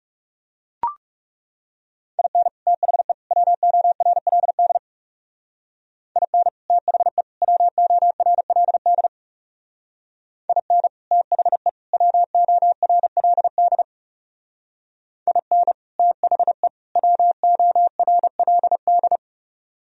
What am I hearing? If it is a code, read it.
Morse code, heard as a keyed tone, about 37 wpm, 700 Hz. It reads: IN THE WORLD IN THE WORLD IN THE WORLD IN THE WORLD
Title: Speed-Racing - Sets of 3 Words 30wpm